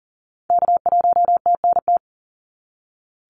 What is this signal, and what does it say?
Morse code, heard as a keyed tone, about 37 wpm, 700 Hz. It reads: K1TNT
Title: NA SOTA Activator Callsigns 40wpm